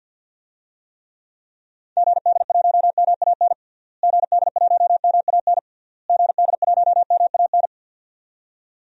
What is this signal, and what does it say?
Morse code, heard as a keyed tone, about 47 wpm, 700 Hz. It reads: GD1MAN GD1MAN GD1MAN
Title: DXCC-100 Call Signs - 6 Characters - Repeated 3x - 50wpm